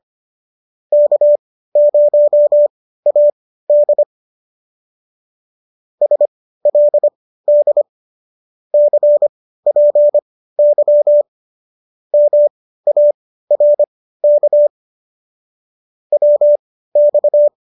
Morse code, heard as a keyed tone, about 25 wpm, spaced as for 15 wpm, 590 Hz. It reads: K0AD SLD CPY MARK WX
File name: qso213 15